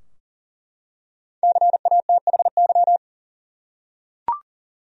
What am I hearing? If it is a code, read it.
Morse code, heard as a keyed tone, about 37 wpm, 700 Hz. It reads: CATHY
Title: USA Common Names - 40wpm